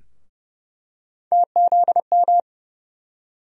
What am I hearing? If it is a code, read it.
Morse code, heard as a keyed tone, about 31 wpm, 700 Hz. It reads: TZM
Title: USA Call Sign Postfixes - 3 Letters 30wpm